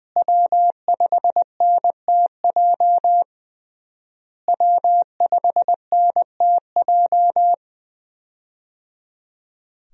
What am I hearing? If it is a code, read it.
Morse code, heard as a keyed tone, about 20 wpm, 700 Hz. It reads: W5NTJ W5NTJ
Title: USA Call Signs - 5 Characters - Repeated 3x - 20wpm